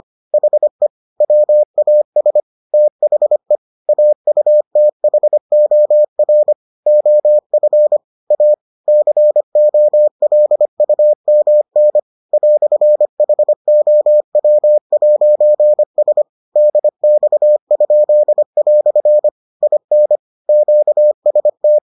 Morse code, heard as a keyed tone, about 25 wpm, 600 Hz. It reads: HE WAS THE AUTHOR OF A COLUMN "HOW'S DX?" IN QST